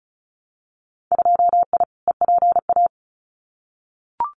Morse code, heard as a keyed tone, about 36 wpm, 700 Hz. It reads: 2I EPA
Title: Winter Field Day Exchanges Encore 35wpm